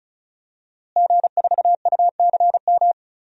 Morse code, heard as a keyed tone, about 34 wpm, 700 Hz. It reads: G4UCM
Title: DXCC-100 Call Signs - 5 Characters - Repeated 2x - 35wpm